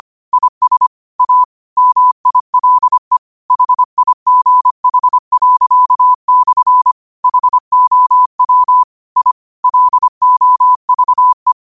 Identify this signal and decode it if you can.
Morse code, heard as a keyed tone, about 25 wpm, 1000 Hz. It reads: IS A MILE HIGH./ HOW I LOVE
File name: qotdfiles / 25wpm